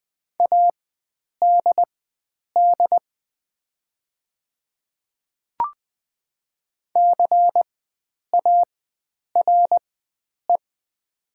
Morse code, heard as a keyed tone, about 20 wpm, spaced as for 9 wpm, 700 Hz. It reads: ADD CARE
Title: 4x Character Spacing - Top 300 Words - 20wpm